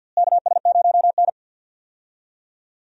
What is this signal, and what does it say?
Morse code, heard as a keyed tone, about 52 wpm, 700 Hz. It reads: KS0N
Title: USA Call Signs - 4 Characters - Repeated 2x - 50wpm